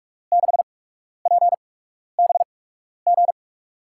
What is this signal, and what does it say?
Morse code, heard as a keyed tone, about 44 wpm, spaced as for 12 wpm, 700 Hz. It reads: 6PBG
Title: USA Call Signs - 8x Character Spacing - 45wpm